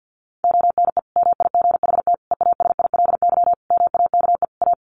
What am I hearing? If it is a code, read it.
Morse code, heard as a keyed tone, about 50 wpm, 700 Hz. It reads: ONE MIGHT EASILY MAKE A
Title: Sentences from Top 1400 Words 50wpm